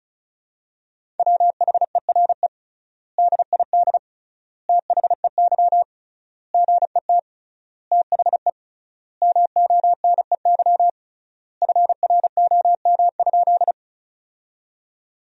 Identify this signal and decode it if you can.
Morse code, heard as a keyed tone, about 36 wpm, 700 Hz. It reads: WHERE DID THEY GET THE MONEY FROM?
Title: 3x Word-Spacing - Sentences from Top 400 Words - 35wpm